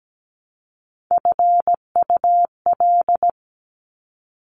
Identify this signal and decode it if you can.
Morse code, heard as a keyed tone, about 17 wpm, 700 Hz. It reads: FUL
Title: Top Three Letter Combinations in Words 17wpm